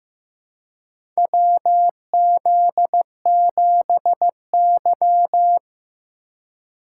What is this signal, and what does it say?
Morse code, heard as a keyed tone, about 15 wpm, 700 Hz. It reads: WZ7Y
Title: USA Call Signs - 4 Characters - 3x Character Spacing - 15wpm